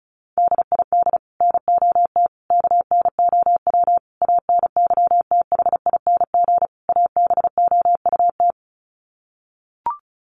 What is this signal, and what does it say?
Morse code, heard as a keyed tone, about 34 wpm, 700 Hz. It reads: DID NOT KNOW ANYTHING ABOUT
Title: Sets of 5 Words Encore2 35wpm